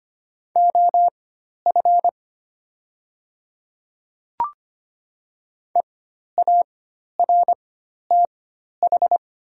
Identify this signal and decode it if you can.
Morse code, heard as a keyed tone, about 25 wpm, spaced as for 12 wpm, 700 Hz. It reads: OF EARTH
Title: 4x Character Spacing - Top 200 Words - 25wpm